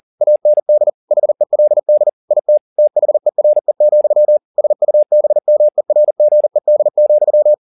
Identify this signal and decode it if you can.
Morse code, heard as a keyed tone, about 40 wpm, 600 Hz. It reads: AND HELD IT THERE, SUBMERGED,